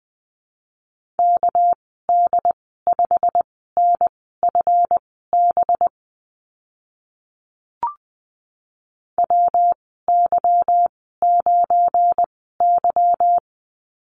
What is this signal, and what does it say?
Morse code, heard as a keyed tone, about 20 wpm, spaced as for 15 wpm, 700 Hz. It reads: KD5NFB WY9Y
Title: 207 Lesson 33g - Letter j - Call Signs 20wpm